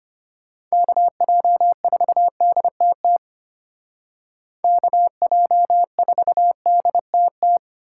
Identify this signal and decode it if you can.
Morse code, heard as a keyed tone, about 30 wpm, 700 Hz. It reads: KJ4DTT KJ4DTT
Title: Speed Racing - USA Call Signs 25wpm